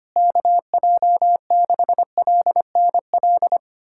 Morse code, heard as a keyed tone, about 25 wpm, 700 Hz. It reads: KJ6LNL
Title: USA Callsign DE Callsign - 6 Characters - 25wpm